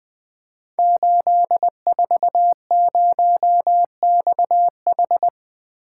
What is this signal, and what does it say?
Morse code, heard as a keyed tone, about 20 wpm, 700 Hz. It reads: 840XH
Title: Code Groups - 5 Letter:Numbers - 20wpm